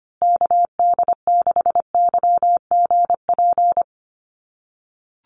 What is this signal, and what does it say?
Morse code, heard as a keyed tone, about 25 wpm, 700 Hz. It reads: KD6YGP
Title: USA Call Signs Repeated 3x 25wpm